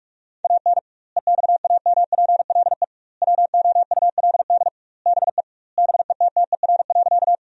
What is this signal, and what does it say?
Morse code, heard as a keyed tone, about 46 wpm, 700 Hz. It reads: AN EXAMPLE WOULD BE BETTER.
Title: 2x Word-Spacing - Sentences from Top 400 Words - 45wpm